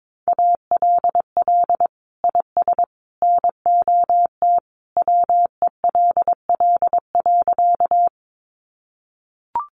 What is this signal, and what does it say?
Morse code, heard as a keyed tone, about 22 wpm, 700 Hz. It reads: ALL IS NOT WELL.
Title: Sentences from Top 600 Words 22wpm